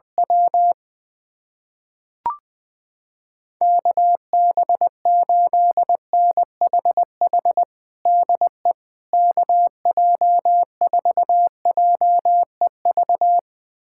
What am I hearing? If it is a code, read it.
Morse code, heard as a keyed tone, about 21 wpm, 700 Hz. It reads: W KB8NHH DE KJ4JEV
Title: USA Callsign DE Callsign 20wpm